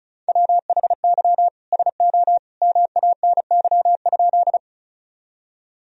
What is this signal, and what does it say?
Morse code, heard as a keyed tone, about 36 wpm, 700 Hz. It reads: WHY SO MANY?
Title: Sentences from Top 1000 Words - Review Part 1 - 35wpm